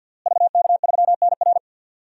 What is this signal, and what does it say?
Morse code, heard as a keyed tone, about 49 wpm, 700 Hz. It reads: VK3NR
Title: DXCC-100 Call Signs - 5 Characters - Repeated 3x - 50wpm